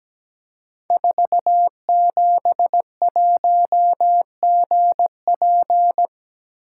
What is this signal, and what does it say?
Morse code, heard as a keyed tone, about 17 wpm, 700 Hz. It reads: 471GP
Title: Code Groups - 5 Letter:Numbers - 17wpm